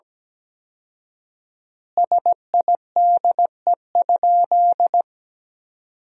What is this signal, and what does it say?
Morse code, heard as a keyed tone, about 17 wpm, 700 Hz. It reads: SIDE?
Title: 4x Word-Spacing - Sentences from Top 400 Words - 17wpm